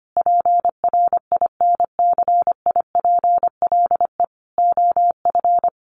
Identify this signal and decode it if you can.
Morse code, heard as a keyed tone, about 25 wpm, 700 Hz. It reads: PRINCIPLE OF
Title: Sentences from Top 1400 Words 25wpm